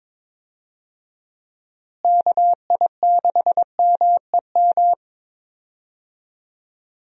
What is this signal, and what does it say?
Morse code, heard as a keyed tone, about 22 wpm, 700 Hz. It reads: KI6MEM
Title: USA Call Signs - 6 Characters - Repeated 3x - 22wpm